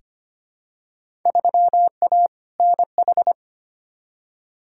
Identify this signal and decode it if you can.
Morse code, heard as a keyed tone, about 25 wpm, 700 Hz. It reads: 3A NH
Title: Field Day Exchanges Encore - 25wpm